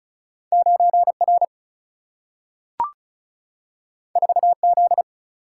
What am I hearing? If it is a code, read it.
Morse code, heard as a keyed tone, about 36 wpm, 700 Hz. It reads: 9R 4Z
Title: Single-Letter Single-Number 35wpm